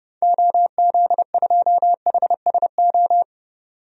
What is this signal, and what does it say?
Morse code, heard as a keyed tone, about 30 wpm, 700 Hz. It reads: OZ2HSO
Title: DXCC-100 Call Signs - 6 Characters - 30wpm